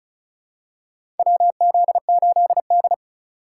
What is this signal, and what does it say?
Morse code, heard as a keyed tone, about 36 wpm, 700 Hz. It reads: WZ8D
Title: USA Call Signs - 4 Characters - 4x Character Spacing - 35wpm